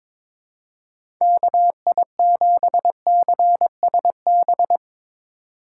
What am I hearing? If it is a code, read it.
Morse code, heard as a keyed tone, about 22 wpm, 700 Hz. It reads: KI7CSB
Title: USA Call Signs - 6 Characters - 4x Character Spacing - 22wpm